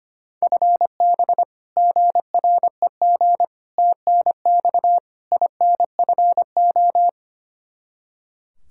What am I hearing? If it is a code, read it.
Morse code, heard as a keyed tone, about 25 wpm, 700 Hz. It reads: FB GREG TNX INFO
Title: Practice QSO - Exchange 3 - More Info - 25wpm